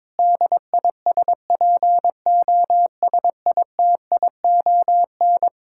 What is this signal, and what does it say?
Morse code, heard as a keyed tone, about 22 wpm, 700 Hz. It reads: DISPOSITION
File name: Difficult Sentences with Hints - 22wpm